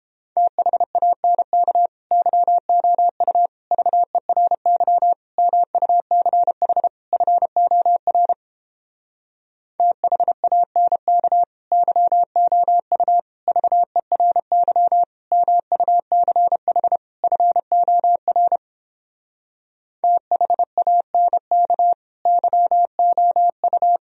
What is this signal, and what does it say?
Morse code, heard as a keyed tone, about 31 wpm, 700 Hz. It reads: THANK YOU VERY MUCH FOR THANK YOU VERY MUCH FOR THANK YOU